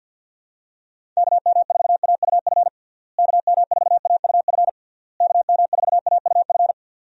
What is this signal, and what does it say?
Morse code, heard as a keyed tone, about 48 wpm, 700 Hz. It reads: KM4AUF KM4AUF KM4AUF
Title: USA Call Signs - 6 Characters - Repeated 3x - 50wpm